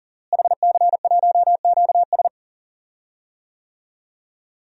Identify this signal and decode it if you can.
Morse code, heard as a keyed tone, about 40 wpm, 700 Hz. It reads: HC1QS
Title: DXCC-100 Call Signs - 5 Characters - Repeated 3x - 40wpm